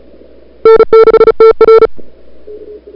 Morse code, heard as a keyed tone, about 35 wpm, 440 Hz. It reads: N6TR
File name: n6tr / Fab2007CWSprintHour1